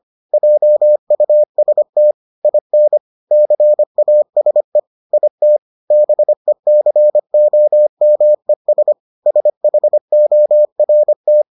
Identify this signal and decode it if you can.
Morse code, heard as a keyed tone, about 25 wpm, 600 Hz. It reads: JUST IN CASE IT BECOMES SHORT